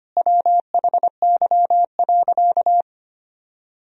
Morse code, heard as a keyed tone, about 25 wpm, 700 Hz. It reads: WHY.
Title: Sentences from Top 1000 Words - Review Part 3 - 25wpm